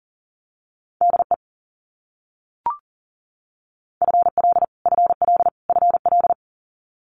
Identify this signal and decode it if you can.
Morse code, heard as a keyed tone, about 40 wpm, 700 Hz. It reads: DE FL FL FL
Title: US States Repeated 3x 40wpm